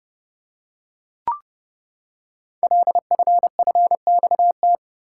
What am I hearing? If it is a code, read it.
Morse code, heard as a keyed tone, about 30 wpm, 700 Hz. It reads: LFFXT
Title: Code Groups - 5 Letters - 30wpm